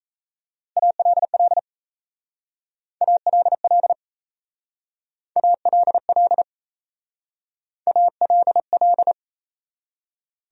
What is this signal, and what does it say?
Morse code, heard as a keyed tone, about 34 wpm, 700 Hz. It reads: ALL ALL ALL ALL
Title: Speed-Racing Top 100 Words 28wpm